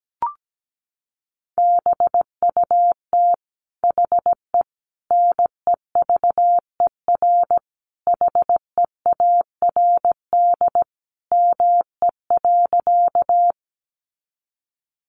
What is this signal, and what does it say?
Morse code, heard as a keyed tone, about 17 wpm, 700 Hz. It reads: BUT HE NEVER HEARD ME.
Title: Sentences from Top 400 Words 17wpm